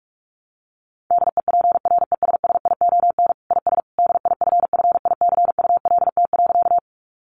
Morse code, heard as a keyed tone, about 44 wpm, 700 Hz. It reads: DEPRESSION IS DIFFICULT.
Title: Sentences from Top 1300 Words 45wpm